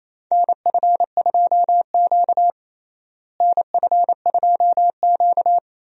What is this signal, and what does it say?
Morse code, heard as a keyed tone, about 28 wpm, 700 Hz. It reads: NF2Q NF2Q
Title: USA Call Signs - 4 Characters - Repeated 3x - 28wpm